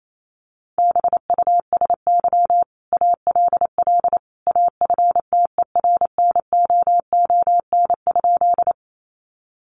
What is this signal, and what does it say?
Morse code, heard as a keyed tone, about 28 wpm, 700 Hz. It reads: BUSY ALL AFTERNOON?
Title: Sentences from Top 1500 Words 28wpm